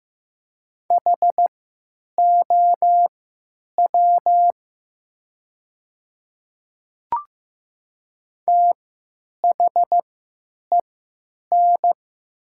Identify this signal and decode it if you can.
Morse code, heard as a keyed tone, about 15 wpm, spaced as for 9 wpm, 700 Hz. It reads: HOW THEN
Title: 3x Character Spacing - Top 100 Words - 15wpm